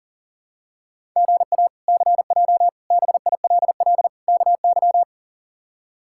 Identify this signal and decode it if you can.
Morse code, heard as a keyed tone, about 40 wpm, 700 Hz. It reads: GA CJ BILL KY
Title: K1USN SST QSO Practice - 40wpm